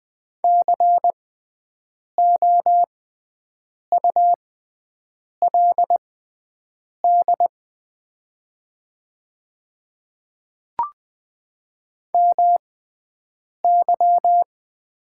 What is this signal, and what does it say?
Morse code, heard as a keyed tone, about 21 wpm, spaced as for 7 wpm, 700 Hz. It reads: COULD MY
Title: 6x Character Spacing - Top 300 Words - 20wpm